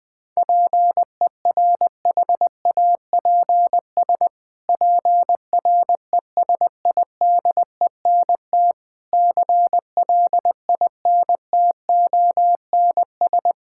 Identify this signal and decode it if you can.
Morse code, heard as a keyed tone, about 20 wpm, 700 Hz. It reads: PERHAPS PRESIDENT CLINTONS
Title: Pangram Challenge v2 - 20wpm